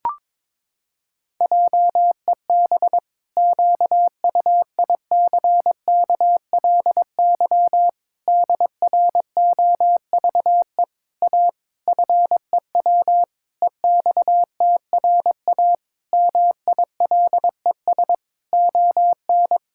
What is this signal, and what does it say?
Morse code, heard as a keyed tone, about 22 wpm, 700 Hz. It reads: JEB QUICKLY DROVE A FEW EXTRA MILES ON
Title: Pangram Challenge v2 - 22wpm